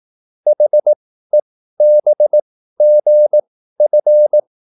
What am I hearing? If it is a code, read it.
Morse code, heard as a keyed tone, about 18 wpm, spaced as for 13 wpm, 600 Hz. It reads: HEBGF